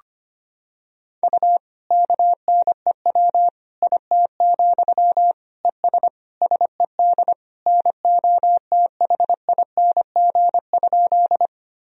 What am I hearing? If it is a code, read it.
Morse code, heard as a keyed tone, about 25 wpm, 700 Hz. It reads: U KNEW IT, ES SED NOTHING?
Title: Sentences from Top 500 Words and Abbreviations 25wpm